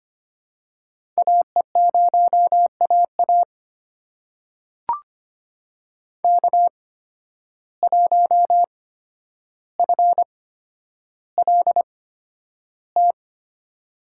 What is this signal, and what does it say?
Morse code, heard as a keyed tone, about 25 wpm, spaced as for 11 wpm, 700 Hz. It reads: AE0AA K1FLT
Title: USA Call Signs - 5 Characters - 8x Character Spacing - 25wpm